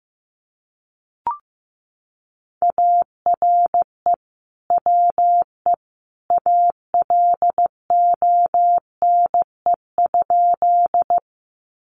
Morse code, heard as a keyed tone, about 15 wpm, 700 Hz. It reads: ARE WE ALONE?
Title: Sentences from Top 1000 Words 15wpm